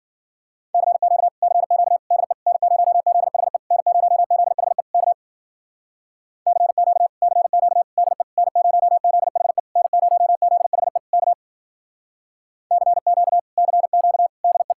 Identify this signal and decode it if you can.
Morse code, heard as a keyed tone, about 54 wpm, 700 Hz. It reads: CQ CQ DE N0ZHE N0ZHE K CQ CQ DE N0ZHE N0ZHE K CQ CQ DE